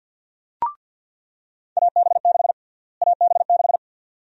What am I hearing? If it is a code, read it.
Morse code, heard as a keyed tone, about 47 wpm, 700 Hz. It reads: AB6 AB6
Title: USA Call Sign Prefixes - Repeated 2x - 50wpm